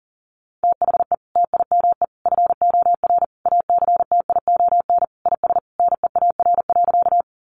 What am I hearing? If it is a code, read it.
Morse code, heard as a keyed tone, about 40 wpm, 700 Hz. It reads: THE TIME FOR ACTION IS NEAR.
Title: Sentences from Top 700 Words 40wpm